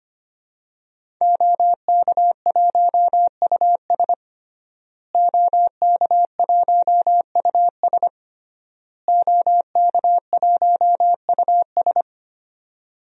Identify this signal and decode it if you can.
Morse code, heard as a keyed tone, about 25 wpm, 700 Hz. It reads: OK1US OK1US OK1US
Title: DXCC-100 Call Signs - 5 Characters - Repeated 3x - 25wpm